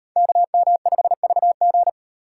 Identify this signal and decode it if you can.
Morse code, heard as a keyed tone, about 36 wpm, 700 Hz. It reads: KM5VG